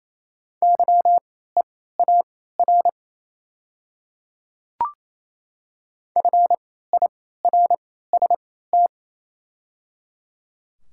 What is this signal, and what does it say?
Morse code, heard as a keyed tone, about 28 wpm, spaced as for 16 wpm, 700 Hz. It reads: YEAR FIRST
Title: 3x Character Spacing - Top 300 Words - 28wpm